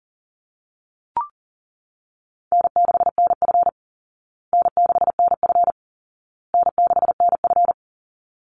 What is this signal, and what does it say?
Morse code, heard as a keyed tone, about 40 wpm, 700 Hz. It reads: N6NF N6NF N6NF
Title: Easy Call Signs Repeated 3x - 40wpm